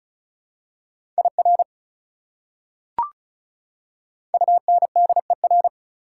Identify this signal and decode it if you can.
Morse code, heard as a keyed tone, about 34 wpm, 700 Hz. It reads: IR UNDER
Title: Prefixes and Suffixes 35wpm